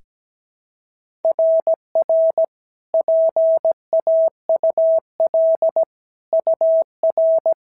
Morse code, heard as a keyed tone, about 16 wpm, 650 Hz. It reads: RR PAUL UR
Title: SOTA QSO Practice 17wpm